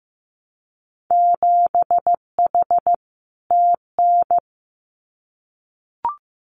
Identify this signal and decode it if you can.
Morse code, heard as a keyed tone, about 15 wpm, 700 Hz. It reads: 7H TN
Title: Winter Field Day Exchanges Encore 15wpm